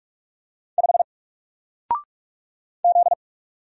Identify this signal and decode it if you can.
Morse code, heard as a keyed tone, about 44 wpm, 700 Hz. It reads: 5 Z
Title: Single Letter-Number Alt 45wpm